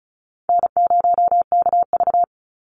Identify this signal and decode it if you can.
Morse code, heard as a keyed tone, about 34 wpm, 700 Hz. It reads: N0KV
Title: Call Signs - 35wpm